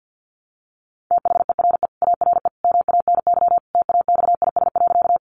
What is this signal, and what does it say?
Morse code, heard as a keyed tone, about 48 wpm, 700 Hz. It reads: THERE ARE MANY TAXIS.
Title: Easy Phrases 50wpm